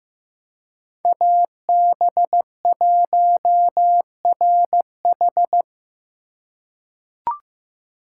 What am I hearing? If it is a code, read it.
Morse code, heard as a keyed tone, about 15 wpm, 700 Hz. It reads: AB1RH
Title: USA Call Signs - 5 Characters - 2x Character Spacing - 15wpm